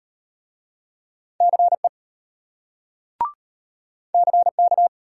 Morse code, heard as a keyed tone, about 34 wpm, 700 Hz. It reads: CE CK